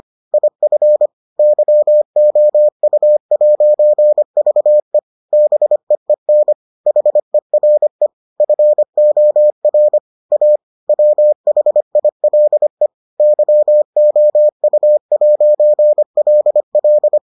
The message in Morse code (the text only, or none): IF YOU'VE BEEN HERE FOR A WHILE YOU'LL